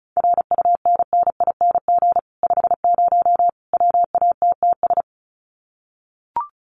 Morse code, heard as a keyed tone, about 34 wpm, 700 Hz. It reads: RUNNING 50 WATTS
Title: Common QSO Phrases 35wpm